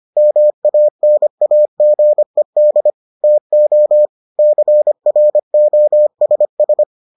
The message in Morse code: MANAGED TO CROSS

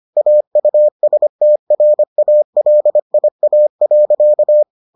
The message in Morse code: AUSTRALIA.

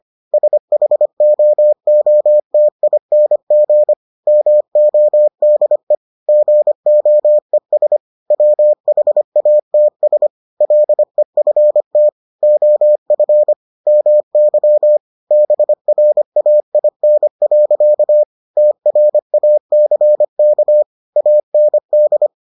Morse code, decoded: SHOOTING MODE GOES WHATS LEFT OF MY BRAIN. TRACK AND

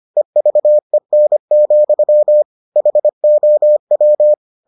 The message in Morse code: EVEN, HOW

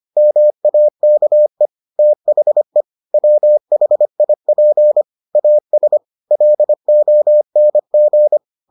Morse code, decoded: MAKE THE WHIP AS LONG

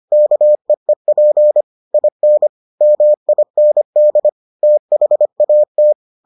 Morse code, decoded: KEEP IN MIND THAT

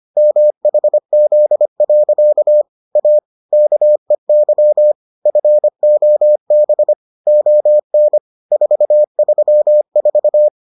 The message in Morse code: MHZ. A KEY FOB ON 434

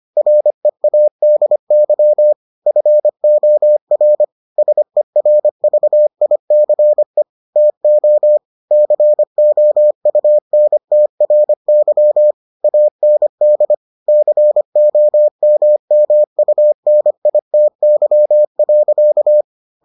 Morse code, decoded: READY FOR SERVICE TO COUNTRY AND COMMUNITY.

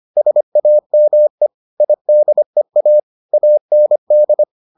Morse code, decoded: SAME IDEA AND

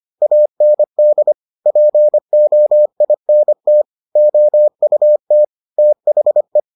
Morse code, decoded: AND POINT OUT THE

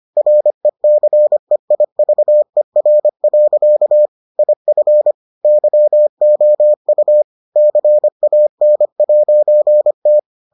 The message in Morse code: RECEIVER. IF YOU CAN'T

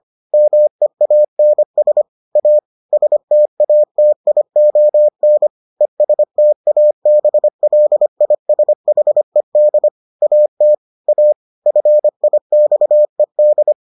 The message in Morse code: MEANS A STATION ESTABLISHED AT A FIXED

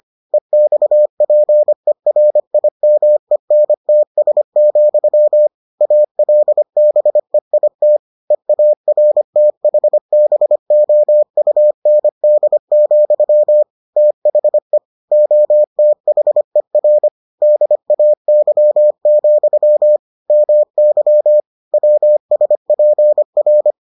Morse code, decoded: EXPERIMENTS, ALBEIT EARTHBOUND, THE OTHER DAY, MY WSPR

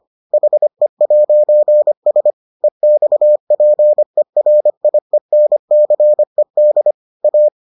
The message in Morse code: HE'S EXPERIENCED A

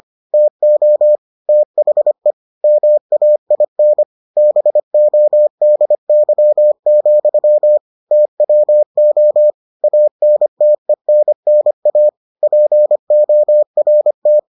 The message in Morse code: TO THE MAIN BODY, TWO ANTENNA PORT

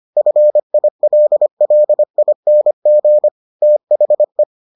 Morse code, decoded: FILLING THE

